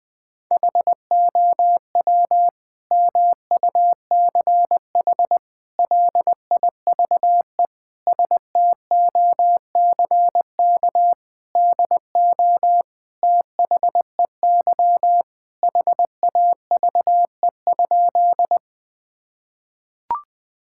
Morse code, heard as a keyed tone, about 21 wpm, 700 Hz. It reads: HOW MUCH LIVE STOCK DO THEY HAVE?
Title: Sentences from Top 1000 Words - Review Part 5 - 20wpm